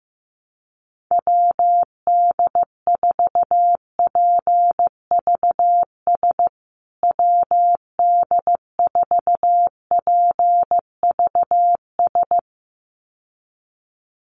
Morse code, heard as a keyed tone, about 15 wpm, 700 Hz. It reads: WD4PVS WD4PVS
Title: USA Call Signs Repeated 2x 15wpm